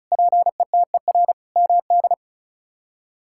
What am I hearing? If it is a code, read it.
Morse code, heard as a keyed tone, about 36 wpm, 700 Hz. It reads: PETER MD
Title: K1USN SST QSO Practice - 35wpm